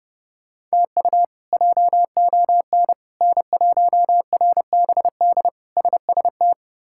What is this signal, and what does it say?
Morse code, heard as a keyed tone, about 32 wpm, 700 Hz. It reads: TU JON N1RBD SST
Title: K1USN SST QSO Practice - 30wpm